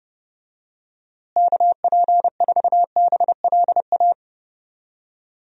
Morse code, heard as a keyed tone, about 30 wpm, 700 Hz. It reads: KP4BLA
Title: DXCC-100 Call Signs - 6 Characters - Repeated 3x - 30wpm